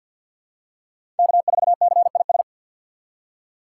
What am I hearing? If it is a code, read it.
Morse code, heard as a keyed tone, about 48 wpm, 700 Hz. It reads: K4CIS
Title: USA Call Signs - 6x Character Spacing - 50wpm